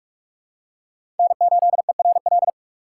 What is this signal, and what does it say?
Morse code, heard as a keyed tone, about 44 wpm, 700 Hz. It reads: N8ERL
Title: USA Call Signs - 5 Characters - 3x Character Spacing - 45wpm